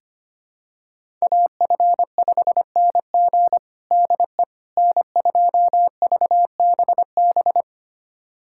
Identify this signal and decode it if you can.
Morse code, heard as a keyed tone, about 25 wpm, 700 Hz. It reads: AF5NG DE N2VBB
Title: USA Callsign DE Callsign - 5 Characters - 25wpm